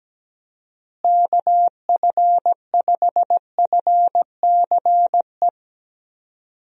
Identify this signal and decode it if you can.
Morse code, heard as a keyed tone, about 17 wpm, 700 Hz. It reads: KF5FCE
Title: USA Call Signs - 6 Characters - Repeated 2x - 17wpm